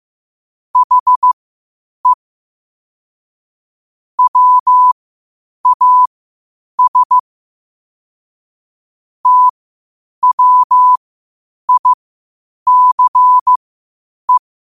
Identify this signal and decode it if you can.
Morse code, heard as a keyed tone, about 15 wpm, spaced as for 8 wpm, 1000 Hz. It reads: HE WAS TWICE
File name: qotdfiles / 8wpm